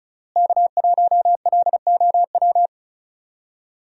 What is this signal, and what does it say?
Morse code, heard as a keyed tone, about 36 wpm, 700 Hz. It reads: K1LOW
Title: USA Call Signs - 5 Characters - Repeated 3x - 35wpm